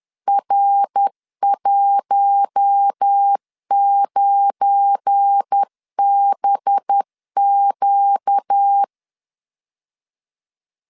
Morse code, heard as a keyed tone, about 10 wpm, 800 Hz. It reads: R19BQ
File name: LEZIONI / 7 GF9Z / 7 GF9 10wpm 2747